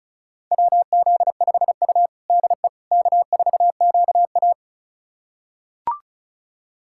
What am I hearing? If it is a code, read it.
Morse code, heard as a keyed tone, about 36 wpm, 690 Hz. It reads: WZ5U DE K4QA
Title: USA Callsign DE Callsign - 4 Characters - 35wpm